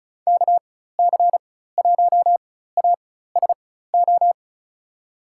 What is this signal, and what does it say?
Morse code, heard as a keyed tone, about 36 wpm, spaced as for 16 wpm, 700 Hz. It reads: KC1ASO
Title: USA Call Signs - 6 Characters - 4x Character Spacing - 35wpm